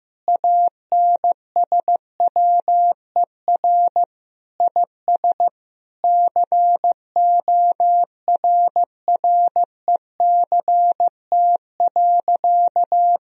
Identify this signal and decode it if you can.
Morse code, heard as a keyed tone, about 15 wpm, 700 Hz. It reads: ANSWER IS CORRECT.
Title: Sentences from Top 1000 Words - Review Part 2 - 15wpm